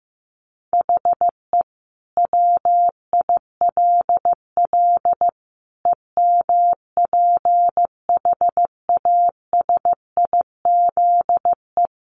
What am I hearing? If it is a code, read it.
Morse code, heard as a keyed tone, about 15 wpm, 700 Hz. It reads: HE WILL EMPHASIZE
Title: Sentences from Top 1500 Words 15wpm